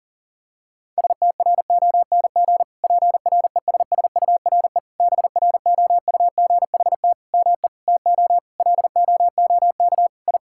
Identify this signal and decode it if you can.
Morse code, heard as a keyed tone, about 40 wpm, 700 Hz. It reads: STRONG PRESSURE BROUGHT ME TO LOOK I
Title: Sentences from Top 1000 Words - Review Part 6 - 40wpm